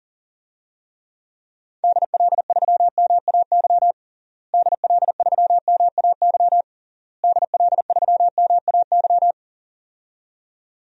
Morse code, heard as a keyed tone, about 40 wpm, 700 Hz. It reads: DL3MAY DL3MAY DL3MAY
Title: DXCC-100 Call Signs - 6 Characters - Repeated 3x - 40wpm